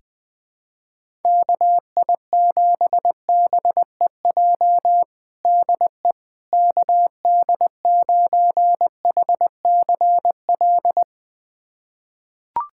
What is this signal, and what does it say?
Morse code, heard as a keyed tone, about 20 wpm, 700 Hz. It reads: KI7BEJ DE KD9HCL
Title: USA Callsign DE Callsign - 6 Characters - 20wpm